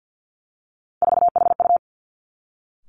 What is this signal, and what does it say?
Morse code, heard as a keyed tone, about 50 wpm, 700 Hz. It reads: 4HU
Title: Three Challenging Letter-Number Combinations 50wpm